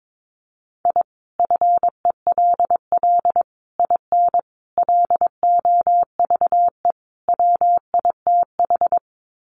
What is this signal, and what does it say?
Morse code, heard as a keyed tone, about 22 wpm, 700 Hz. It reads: I FELL IN LOVE WITH
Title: Sets of 5 Words Encore 22wpm